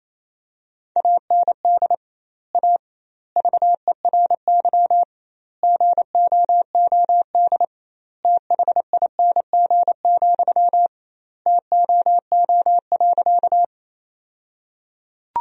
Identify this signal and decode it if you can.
Morse code, heard as a keyed tone, about 28 wpm, 700 Hz. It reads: AND A VERY GOOD THING, TOO.
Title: 2x Word-Spacing - Sentences from Top 200 Words - 28wpm